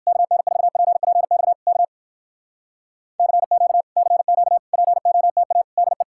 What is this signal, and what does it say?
Morse code, heard as a keyed tone, about 69 wpm, 700 Hz. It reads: KN4PPX K CQ CQ POTA DE